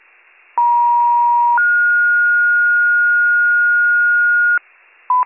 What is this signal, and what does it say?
Two-tone page, heard as a one-second tone, then a three-second tone, about 950 then 1430 Hz, with noise. It is heard from 0.6 to 4.6 s.